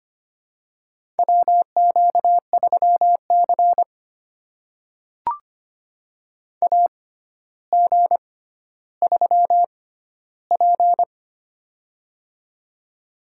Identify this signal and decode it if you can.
Morse code, heard as a keyed tone, about 25 wpm, spaced as for 13 wpm, 700 Hz. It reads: WQ3C AG3P
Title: USA Call Signs - 4 Characters - 6x Character Spacing - 25wpm